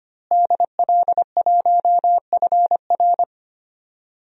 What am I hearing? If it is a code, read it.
Morse code, heard as a keyed tone, about 25 wpm, 700 Hz. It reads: DL1FR
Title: DXCC-100 Call Signs - 5 Characters - Repeated 2x - 25wpm